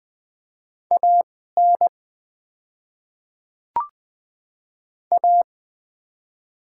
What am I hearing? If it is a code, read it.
Morse code, heard as a keyed tone, about 20 wpm, spaced as for 15 wpm, 700 Hz. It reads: AN A
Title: 2x Character Spacing - Top 100 Words - 20wpm